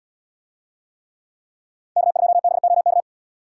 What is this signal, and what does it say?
Morse code, heard as a keyed tone, about 72 wpm, 700 Hz. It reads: K2BOK